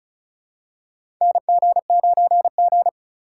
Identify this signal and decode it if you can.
Morse code, heard as a keyed tone, about 36 wpm, 700 Hz. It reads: NG9G
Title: USA Call Signs - 3x Character Spacing - 35wpm